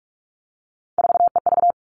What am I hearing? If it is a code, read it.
Morse code, heard as a keyed tone, about 46 wpm, 700 Hz. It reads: 4EV
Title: Three Challenging Letter-Number Combinations 45wpm